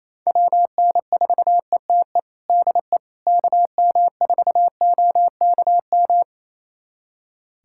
Morse code, heard as a keyed tone, about 28 wpm, 700 Hz. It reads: WN4ETE DE KM4OKM
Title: USA Callsign DE Callsign 28wpm